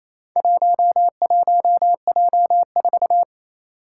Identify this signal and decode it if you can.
Morse code, heard as a keyed tone, about 28 wpm, 700 Hz. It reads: 11J4
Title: Four Challenging Letter-Number Combinations 28wpm